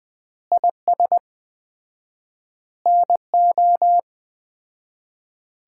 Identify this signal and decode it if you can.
Morse code, heard as a keyed tone, about 21 wpm, 700 Hz. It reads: IS NO
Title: 4x Word-Spacing - Sets of 5 Words - 20wpm